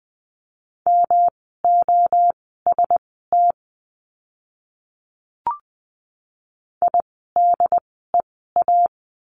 Morse code, heard as a keyed tone, about 20 wpm, spaced as for 14 wpm, 700 Hz. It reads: MOST IDEA
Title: 108 Lesson 18f - Letter m - Words 20wpm